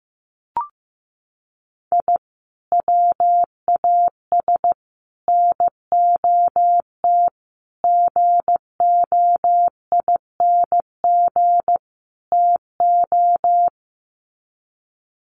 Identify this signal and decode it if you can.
Morse code, heard as a keyed tone, about 15 wpm, 700 Hz. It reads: I WAS NOT GOING TO
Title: Sets of 5 Words 15wpm